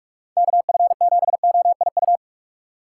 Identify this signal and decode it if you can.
Morse code, heard as a keyed tone, about 46 wpm, 700 Hz. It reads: KF7OIU
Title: USA Call Signs - 6 Characters - 4x Character Spacing - 45wpm